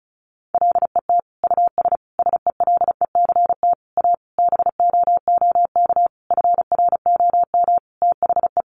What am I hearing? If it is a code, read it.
Morse code, heard as a keyed tone, about 34 wpm, 700 Hz. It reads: LET US SELECT A BOOK FROM THE
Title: Sentences from Top 1300 Words 35wpm